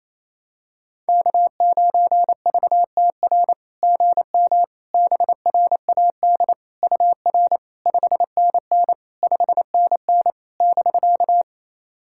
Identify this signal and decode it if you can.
Morse code, heard as a keyed tone, about 28 wpm, 700 Hz. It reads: K9VTR GM BRAD UR 5NN 5NN <BK>
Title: SOTA QSO Practice 28wpm